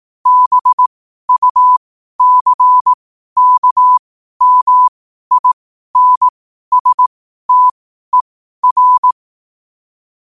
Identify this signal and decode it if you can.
Morse code, heard as a keyed tone, about 18 wpm, spaced as for 12 wpm, 1000 Hz. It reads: BUCKMINSTER
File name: qotdfiles / 12wpm